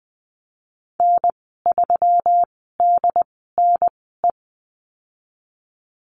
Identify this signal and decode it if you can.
Morse code, heard as a keyed tone, about 20 wpm, spaced as for 15 wpm, 700 Hz. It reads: N3DNE
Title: 161 Lesson 26g - Letter g - Call Signs 20wpm